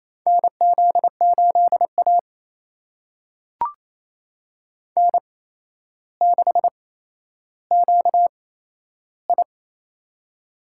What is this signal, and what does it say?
Morse code, heard as a keyed tone, about 27 wpm, spaced as for 12 wpm, 700 Hz. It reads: NZ8A N6QI
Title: USA Call Signs - 4 Characters - 8x Character Spacing - 28wpm